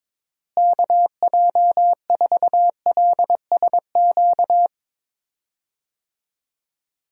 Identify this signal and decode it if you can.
Morse code, heard as a keyed tone, about 22 wpm, 700 Hz. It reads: KJ4LSQ
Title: USA Call Signs - 6 Characters - Repeated 2x - 22wpm